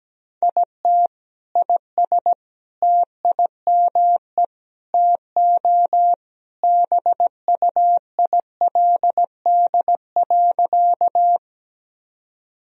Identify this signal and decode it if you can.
Morse code, heard as a keyed tone, about 17 wpm, 700 Hz. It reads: IT IS TIME TO BUILD.
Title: Sentences from Top 1000 Words - Review Part 1 - 17wpm